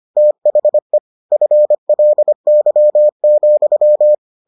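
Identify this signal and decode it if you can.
Morse code, heard as a keyed tone, about 25 wpm, 600 Hz. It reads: THE FLY,
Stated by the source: Morse edition of the Foundations of Amateur Radio episodes.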